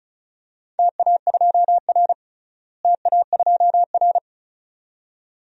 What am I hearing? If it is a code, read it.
Morse code, heard as a keyed tone, about 36 wpm, 700 Hz. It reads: TA2R TA2R
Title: DXCC-100 Call Signs - 4 Characters - Repeated 2x - 35wpm